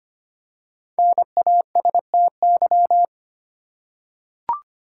Morse code, heard as a keyed tone, about 25 wpm, 700 Hz. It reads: NASTY
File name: 5 Letter Words - First 3 Letters Given - 25wpm